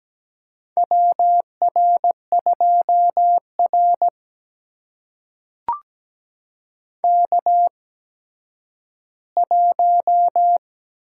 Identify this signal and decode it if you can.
Morse code, heard as a keyed tone, about 17 wpm, spaced as for 10 wpm, 700 Hz. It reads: WR2R K1
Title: USA Call Signs - 4 Characters - 8x Character Spacing - 17wpm